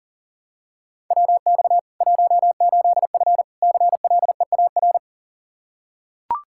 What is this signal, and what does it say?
Morse code, heard as a keyed tone, about 40 wpm, 700 Hz. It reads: WX 18F CLEAR
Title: Practice QSO - Exchange 2 - Info - 40wpm